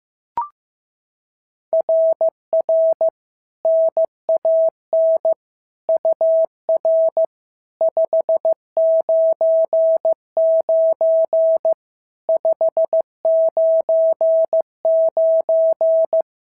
Morse code, heard as a keyed tone, about 15 wpm, 650 Hz. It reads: RR NAN UR 599 599
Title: SOTA QSO Practice 15wpm